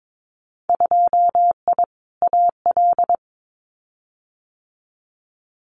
Morse code, heard as a keyed tone, about 22 wpm, 700 Hz. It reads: 2I AL
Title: Winter Field Day Exchanges 22wpm